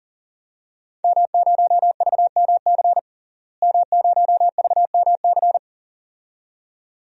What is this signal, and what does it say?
Morse code, heard as a keyed tone, about 40 wpm, 700 Hz. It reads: M0VMC M0VMC
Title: DXCC-100 Call Signs - 5 Characters - Repeated 2x - 40wpm